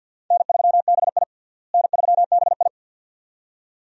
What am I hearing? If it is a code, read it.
Morse code, heard as a keyed tone, about 52 wpm, 700 Hz. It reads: N3BI N3BI
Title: USA Call Signs - 4 Characters - Repeated 2x - 50wpm